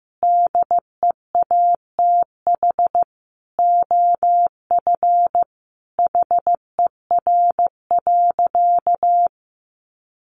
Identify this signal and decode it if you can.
Morse code, heard as a keyed tone, about 15 wpm, 700 Hz. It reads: DEATH OF HER.
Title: Sentences from Top 1100 Words 15wpm